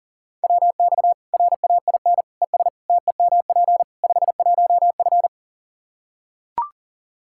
Morse code, heard as a keyed tone, about 40 wpm, 700 Hz. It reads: WX RAIN ES TEMP 51F
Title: Practice QSO - Exchange 2 - Info - 40wpm